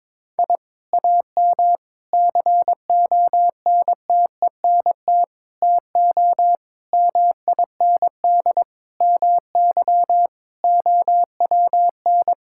Morse code, heard as a keyed tone, about 22 wpm, 700 Hz. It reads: I AM CONTENT TO MIND MY OWN
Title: Sentences from Top 1000 Words - Review Part 6 - 22wpm